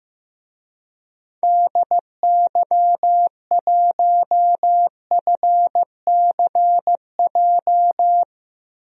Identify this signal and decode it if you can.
Morse code, heard as a keyed tone, about 15 wpm, 700 Hz. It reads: DY1FCJ
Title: DXCC-100 Call Signs - 6 Characters - Repeated 3x - 15wpm